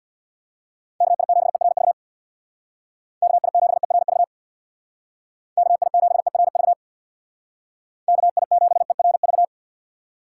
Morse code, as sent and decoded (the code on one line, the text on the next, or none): -.- .. --... .-. ...- / -.- .. --... .-. ...- / -.- .. --... .-. ...- / -.- .. --... .-. ...-
KI7ERV KI7ERV KI7ERV KI7ERV